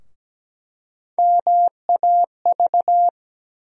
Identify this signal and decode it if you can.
Morse code, heard as a keyed tone, about 17 wpm, 700 Hz. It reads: MAV
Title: USA Call Sign Postfixes - 3 Letters 17wpm